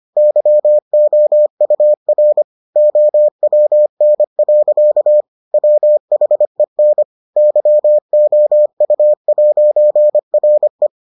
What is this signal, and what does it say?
Morse code, heard as a keyed tone, about 25 wpm, 600 Hz. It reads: YOUR OWN. WHEN YOU'RE